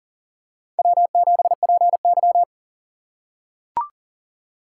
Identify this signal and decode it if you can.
Morse code, heard as a keyed tone, about 40 wpm, 700 Hz. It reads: W7PY
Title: USA Call Signs - 2x Character Spacing - 40wpm